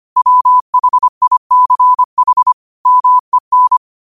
Morse code, heard as a keyed tone, about 25 wpm, 1000 Hz. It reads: WHICH MEN
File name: qotdfiles / 25wpm